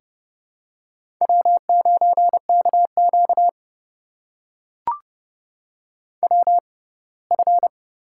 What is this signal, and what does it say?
Morse code, heard as a keyed tone, about 30 wpm, spaced as for 20 wpm, 700 Hz. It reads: W9KQ WF
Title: USA Call Signs - 4 Characters - 6x Character Spacing - 30wpm